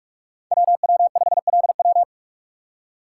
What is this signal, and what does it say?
Morse code, heard as a keyed tone, about 44 wpm, 700 Hz. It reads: WW5LW
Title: USA Call Signs - 5 Characters - 6x Character Spacing - 45wpm